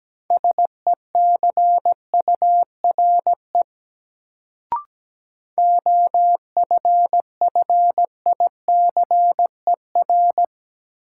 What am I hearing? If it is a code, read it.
Morse code, heard as a keyed tone, about 17 wpm, 700 Hz. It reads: SECURE OFFICER
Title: Sentences from Top 1200 Words - New Word Drill - 17wpm